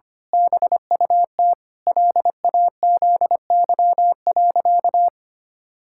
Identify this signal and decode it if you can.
Morse code, heard as a keyed tone, about 25 wpm, 700 Hz. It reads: BUT LAZY.
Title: Pangram Challenge v2 - 25wpm